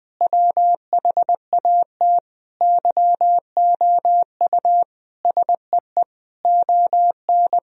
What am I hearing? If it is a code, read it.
Morse code, heard as a keyed tone, about 20 wpm, 700 Hz. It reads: WHAT YOU SEE ON
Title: Sentences from Top 1000 Words - Review Part 4 - 20wpm